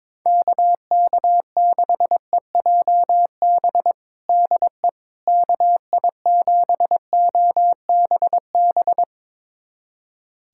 Morse code, heard as a keyed tone, about 22 wpm, 700 Hz. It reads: KK6EJB DE KI7OBB
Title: USA Callsign DE Callsign - 6 Characters - 22wpm